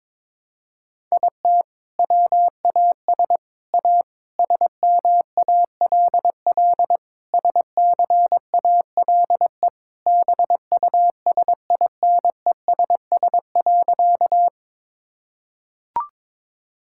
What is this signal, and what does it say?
Morse code, heard as a keyed tone, about 22 wpm, 700 Hz. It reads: IT WAS A SMALL SCALE BUSINESS.
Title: Sentences from Top 1000 Words - Review Part 5 - 22wpm